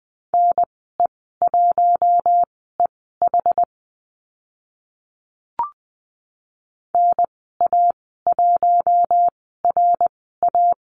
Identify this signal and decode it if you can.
Morse code, heard as a keyed tone, about 20 wpm, spaced as for 14 wpm, 700 Hz. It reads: NE1EH NA1RA
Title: 076 Lesson 13g - Letter L - Call Signs 20wpm